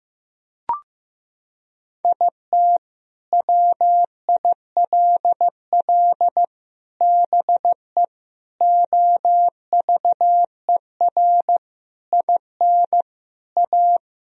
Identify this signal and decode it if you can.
Morse code, heard as a keyed tone, about 15 wpm, 700 Hz. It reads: IT WILL BE OVER IN A
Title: Sentences from Top 1000 Words - Review Part 2 - 15wpm